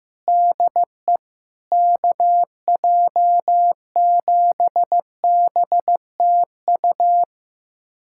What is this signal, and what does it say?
Morse code, heard as a keyed tone, about 15 wpm, 700 Hz. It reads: DE KJ7BTU
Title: USA Callsign DE Callsign - 6 Characters - 15wpm